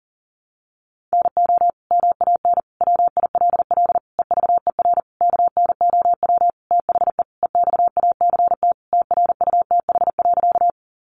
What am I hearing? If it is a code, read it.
Morse code, heard as a keyed tone, about 40 wpm, 700 Hz. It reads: NO MAN WILL EVER KNOW THE EXACT TRUTH.
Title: Sentences from Top 1300 Words 40wpm